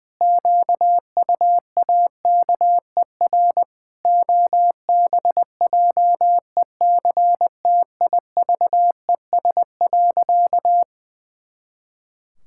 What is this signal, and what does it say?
Morse code, heard as a keyed tone, about 20 wpm, 700 Hz. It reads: QUAKER OBJECTIVES.
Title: Pangram Challenge v2 - 20wpm